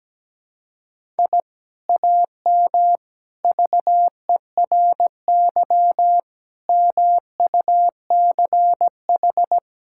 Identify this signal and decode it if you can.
Morse code, heard as a keyed tone, about 17 wpm, 700 Hz. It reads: I AM VERY MUCH
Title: Sentences from Top 1000 Words - Review Part 6 - 17wpm